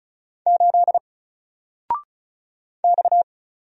Morse code, heard as a keyed tone, about 36 wpm, 700 Hz. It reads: 8 X
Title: Single Character (Letter:Number:Punctuation:Prosign) 35wpm